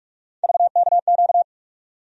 Morse code, heard as a keyed tone, about 46 wpm, 700 Hz. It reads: VKQ
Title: USA Call Sign Postfixes - 3 Letters 45wpm